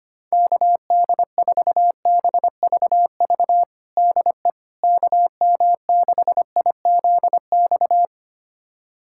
Morse code, heard as a keyed tone, about 25 wpm, 700 Hz. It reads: KD4BVV DE KM6IZX
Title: USA Callsign DE Callsign - 6 Characters - 25wpm